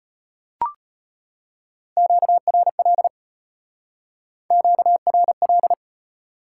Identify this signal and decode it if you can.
Morse code, heard as a keyed tone, about 34 wpm, 700 Hz. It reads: QRL QRL